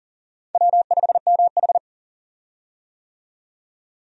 Morse code, heard as a keyed tone, about 40 wpm, 700 Hz. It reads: W5MH
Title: USA Call Signs - 4 Characters - 40wpm